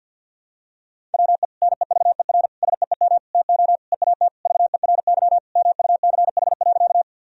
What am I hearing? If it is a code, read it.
Morse code, heard as a keyed tone, about 52 wpm, 700 Hz. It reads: WE NEVER SEEM TO EAT VERY MUCH.
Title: Sentences from Top 1000 Words - Review Part 1 - 50wpm